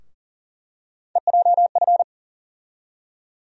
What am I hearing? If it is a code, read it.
Morse code, heard as a keyed tone, about 40 wpm, 700 Hz. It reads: EJF
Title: USA Call Sign Postfixes - 3 Letters 40wpm